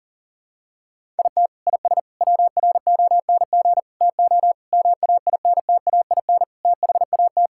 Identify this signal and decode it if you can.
Morse code, heard as a keyed tone, about 37 wpm, 700 Hz. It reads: IT IS WRONG TO MAINTAIN THAT
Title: Sentences from Top 1000 Words - Review Part 5 - 40wpm